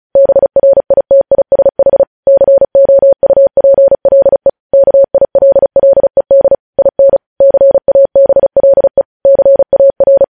Morse code, heard as a keyed tone, about 36 wpm, 550 Hz. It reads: BRITISH COUPLE KILLED IN CABLE CAR
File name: podcast / News headlines-1x-35x35